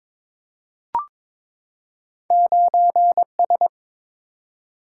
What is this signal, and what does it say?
Morse code, heard as a keyed tone, about 22 wpm, 700 Hz. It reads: 9S
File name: practice / 22wpm / Two Letter-Number Combinations - 22wpm